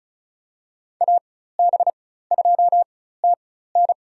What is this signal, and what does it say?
Morse code, heard as a keyed tone, about 34 wpm, spaced as for 16 wpm, 700 Hz. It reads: AB2TN
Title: USA Call Signs - 5 Characters - 4x Character Spacing - 35wpm